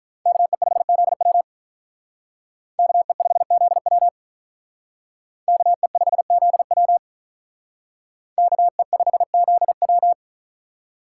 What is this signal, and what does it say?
Morse code, heard as a keyed tone, about 46 wpm, 700 Hz. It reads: KE5ZW KE5ZW KE5ZW KE5ZW